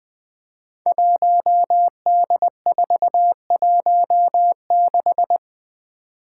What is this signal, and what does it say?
Morse code, heard as a keyed tone, about 20 wpm, 700 Hz. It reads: 1D416
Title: Five Challenging Letter-Number Combinations 20wpm